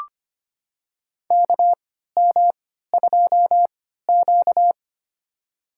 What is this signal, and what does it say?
Morse code, heard as a keyed tone, about 25 wpm, spaced as for 14 wpm, 700 Hz. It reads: KM2Q
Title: USA Call Signs - 4 Characters - 3x Character Spacing - 25wpm